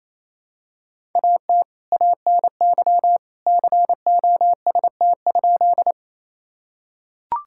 Morse code, heard as a keyed tone, about 28 wpm, 700 Hz. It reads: AT ANY COST?
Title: Sentences from Top 1000 Words - Review Part 5 - 28wpm